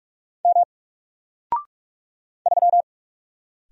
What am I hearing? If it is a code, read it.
Morse code, heard as a keyed tone, about 45 wpm, 700 Hz. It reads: M 3
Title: Single Letter-Number Alt 45wpm